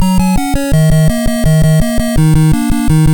There is no code.